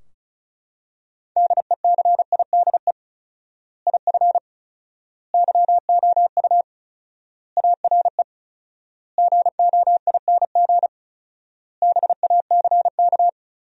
Morse code, heard as a keyed tone, about 34 wpm, 700 Hz. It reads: DECIDE IF YOU ARE GOING BACK
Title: 4x Word-Spacing - Sentences from Top 500 Words - 35wpm